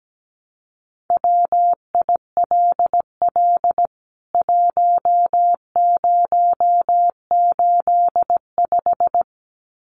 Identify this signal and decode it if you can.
Morse code, heard as a keyed tone, about 17 wpm, 700 Hz. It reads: WILL 1085
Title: CWT Exchange 17wpm